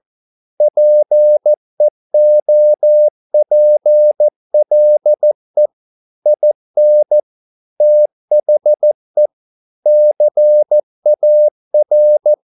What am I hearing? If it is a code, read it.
Morse code, heard as a keyed tone, about 14 wpm, 600 Hz. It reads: PEOPLE IN THE CAR